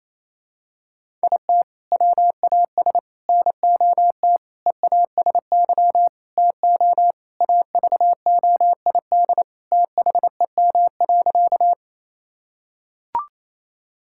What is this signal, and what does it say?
Morse code, heard as a keyed tone, about 28 wpm, 700 Hz. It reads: IT WAS NOT EASY TO AVOID THEM.
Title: Sentences from Top 1000 Words - Review Part 6 - 28wpm